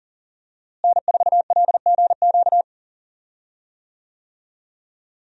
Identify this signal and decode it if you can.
Morse code, heard as a keyed tone, about 37 wpm, 700 Hz. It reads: N4LGQ
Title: USA Call Signs - 5 Characters - Repeated 2x - 40wpm